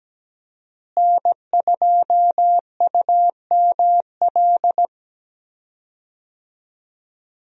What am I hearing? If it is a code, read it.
Morse code, heard as a keyed tone, about 17 wpm, 700 Hz. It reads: N2UML
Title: USA Call Signs - 5 Characters - Repeated 3x - 17wpm